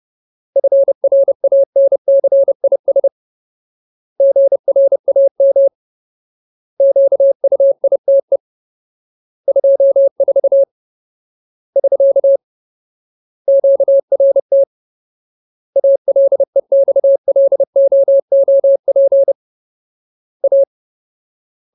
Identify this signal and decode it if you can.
Morse code, heard as a keyed tone, about 30 wpm, 550 Hz. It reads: FRANCIS GRAM QUITE 24 <SK> QRT ALEXLOOP A
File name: Ultimate Word List / 30WPM Ultimate